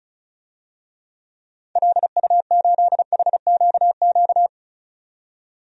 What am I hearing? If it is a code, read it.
Morse code, heard as a keyed tone, about 36 wpm, 700 Hz. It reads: LU8HQQ
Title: DXCC-100 Call Signs - 6 Characters - 35wpm